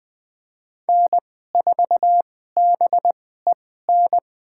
Morse code, heard as a keyed tone, about 20 wpm, spaced as for 15 wpm, 700 Hz. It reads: N4BEN
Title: USA Call Signs - 5 Characters - 2x Character Spacing - 20wpm